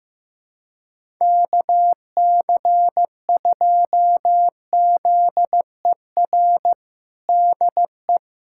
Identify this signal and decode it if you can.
Morse code, heard as a keyed tone, about 15 wpm, 700 Hz. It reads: KC2ZER DE
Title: USA Callsign DE Callsign 15wpm